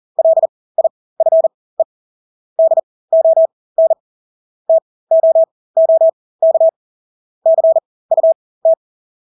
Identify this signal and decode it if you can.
Morse code, heard as a keyed tone, about 40 wpm, spaced as for 20 wpm, 650 Hz. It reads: LIFE DON TOOK CUT
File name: Engelske ord 40-20 wpm